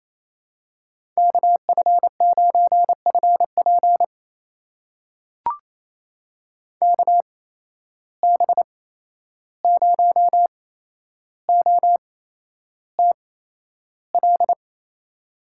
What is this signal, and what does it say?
Morse code, heard as a keyed tone, about 28 wpm, spaced as for 11 wpm, 700 Hz. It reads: KF9FP KB0OTL
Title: USA Call Signs - 8x Character Spacing - 28wpm